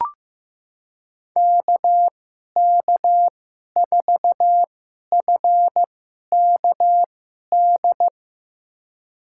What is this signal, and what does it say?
Morse code, heard as a keyed tone, about 15 wpm, spaced as for 11 wpm, 700 Hz. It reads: KK4FKD
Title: USA Call Signs - 2x Character Spacing - 15wpm